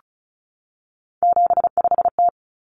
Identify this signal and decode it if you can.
Morse code, heard as a keyed tone, about 34 wpm, 700 Hz. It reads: 75T
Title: Common Cut Numbers - 3-Digit 35wpm